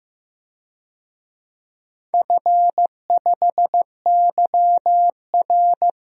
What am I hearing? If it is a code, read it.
Morse code, heard as a keyed tone, about 15 wpm, 700 Hz. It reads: F5YR
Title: DXCC-100 Call Signs - 4 Characters - Repeated 2x - 15wpm